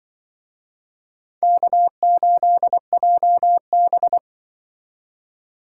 Morse code, heard as a keyed tone, about 25 wpm, 700 Hz. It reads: K8JB